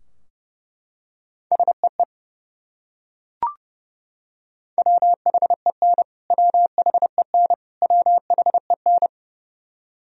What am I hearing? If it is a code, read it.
Morse code, heard as a keyed tone, about 32 wpm, 700 Hz. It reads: SEE WHEN WHEN WHEN
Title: Top 100 Words Repeated 30wpm